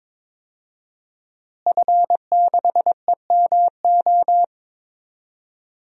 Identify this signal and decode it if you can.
Morse code, heard as a keyed tone, about 22 wpm, 700 Hz. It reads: F6EMO
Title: DXCC-100 Call Signs - 5 Characters - Repeated 3x - 22wpm